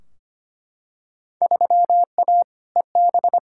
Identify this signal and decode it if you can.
Morse code, heard as a keyed tone, about 25 wpm, 700 Hz. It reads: 3A EB
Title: Field Day Exchanges Encore - 25wpm